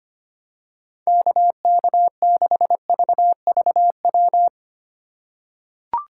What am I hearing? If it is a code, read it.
Morse code, heard as a keyed tone, about 25 wpm, 700 Hz. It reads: KK6VVW
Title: USA Call Signs - 2x Character Spacing - 25wpm